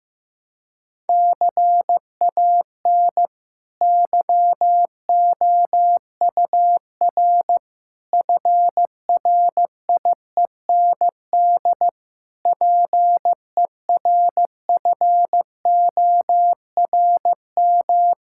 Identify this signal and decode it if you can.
Morse code, heard as a keyed tone, about 15 wpm, 700 Hz. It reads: CAN YOUR FRIEND PERFORM
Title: Sentences from Top 1000 Words - Review Part 5 - 15wpm